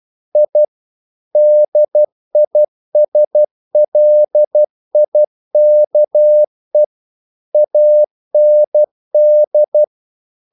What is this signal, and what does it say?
Morse code, heard as a keyed tone, about 12 wpm, 600 Hz. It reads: I DISLIKE AND
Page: English quotations in CW (telegraphy - morse)